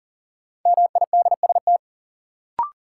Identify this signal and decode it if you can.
Morse code, heard as a keyed tone, about 40 wpm, 700 Hz. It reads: MIDST
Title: Sets of 4 Words - New Word Drill - 40wpm